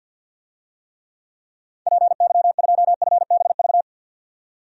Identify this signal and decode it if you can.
Morse code, heard as a keyed tone, about 51 wpm, 700 Hz. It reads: PY2FBV
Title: DXCC-100 Call Signs - 6 Characters - 50wpm